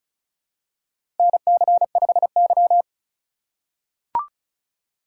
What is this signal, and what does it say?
Morse code, heard as a keyed tone, about 36 wpm, 700 Hz. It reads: NC5Y
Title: USA Call Signs - 4 Characters - 3x Character Spacing - 35wpm